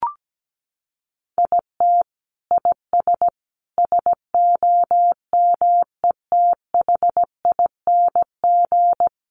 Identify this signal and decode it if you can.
Morse code, heard as a keyed tone, about 17 wpm, 700 Hz. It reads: IT IS SOMETHING
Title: Sentences from Top 1100 Words 17wpm